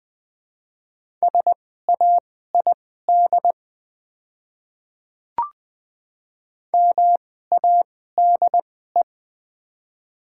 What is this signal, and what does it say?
Morse code, heard as a keyed tone, about 20 wpm, spaced as for 14 wpm, 700 Hz. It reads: SAID MADE
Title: 2x Character Spacing - Top 300 Words - 20wpm